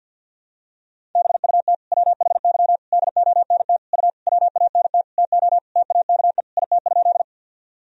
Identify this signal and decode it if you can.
Morse code, heard as a keyed tone, about 49 wpm, 700 Hz. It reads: BUT WHY DONT U WANT TO TAKE IT?
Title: Sentences from Top 300 Words and Abbreviations 50wpm